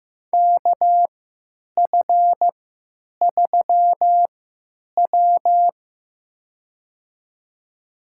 Morse code, heard as a keyed tone, about 15 wpm, spaced as for 9 wpm, 700 Hz. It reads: KF3W
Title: USA Call Signs - 4 Characters - 3x Character Spacing - 15wpm